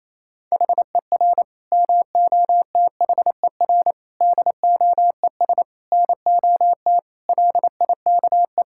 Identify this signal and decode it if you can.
Morse code, heard as a keyed tone, about 27 wpm, 700 Hz. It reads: HER MOTHER DOES NOT LIKE
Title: Sentences from Top 1000 Words - Review Part 5 - 28wpm